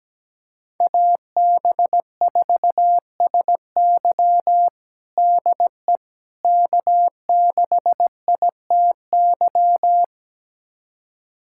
Morse code, heard as a keyed tone, about 17 wpm, 700 Hz. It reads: AB4SY DE K6ITY
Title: USA Callsign DE Callsign - 5 Characters - 17wpm